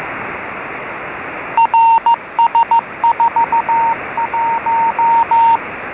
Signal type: Morse code